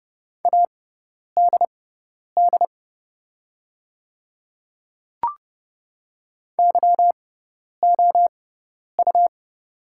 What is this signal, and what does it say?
Morse code, heard as a keyed tone, about 30 wpm, spaced as for 10 wpm, 700 Hz. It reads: ADD YOU
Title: 6x Character Spacing - Top 200 Words - 30wpm